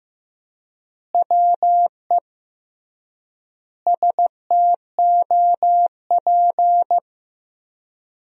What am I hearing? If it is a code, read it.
Morse code, heard as a keyed tone, about 15 wpm, 700 Hz. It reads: WE STOP
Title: 3x Word-Spacing - Sentences from Top 300 Words - 15wpm